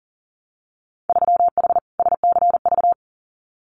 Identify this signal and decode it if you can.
Morse code, heard as a keyed tone, about 42 wpm, 700 Hz. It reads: 3H SCV
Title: Winter Field Day Exchanges Encore 40wpm